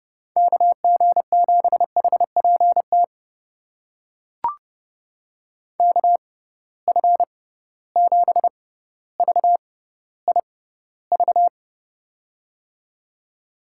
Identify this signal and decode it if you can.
Morse code, heard as a keyed tone, about 30 wpm, spaced as for 15 wpm, 700 Hz. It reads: KG7HPT KF7VIV
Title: USA Call Signs - 6 Characters - 6x Character Spacing - 30wpm